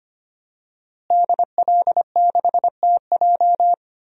Morse code, heard as a keyed tone, about 25 wpm, 700 Hz. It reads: DL6TJ
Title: DXCC-100 Call Signs - 5 Characters - Repeated 3x - 25wpm